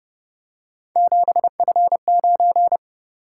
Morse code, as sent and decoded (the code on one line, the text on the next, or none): --... ..-. ----.
7F9